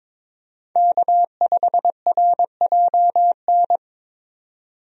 Morse code, heard as a keyed tone, about 22 wpm, 700 Hz. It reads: K5RJN
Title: USA Call Signs - 5 Characters - 22wpm